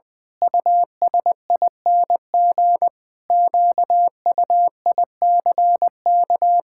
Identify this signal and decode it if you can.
Morse code, heard as a keyed tone, about 20 wpm, 700 Hz. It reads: USING QUICK
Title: Pangram Challenge v2 - 20wpm